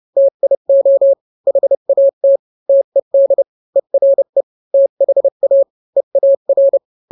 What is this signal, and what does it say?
Morse code, heard as a keyed tone, about 30 wpm, 550 Hz. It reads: TIO HAT TED ERE THA EAR